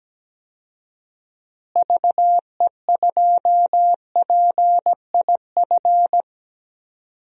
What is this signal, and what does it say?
Morse code, heard as a keyed tone, about 17 wpm, 700 Hz. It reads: VE2PIF
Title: DXCC-100 Call Signs - 6 Characters - Repeated 2x - 17wpm